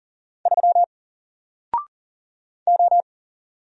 Morse code, heard as a keyed tone, about 40 wpm, 700 Hz. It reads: 3 O
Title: Single Letter-Number 40wpm